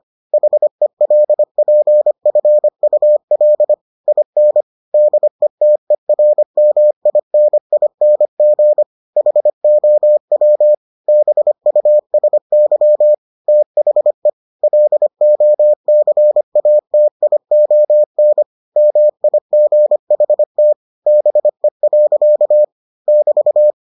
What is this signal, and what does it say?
Morse code, heard as a keyed tone, about 25 wpm, 600 Hz. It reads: HELPFUL IN DETERMINING HOW BUSY THE LOCATION MIGHT BE. <BT>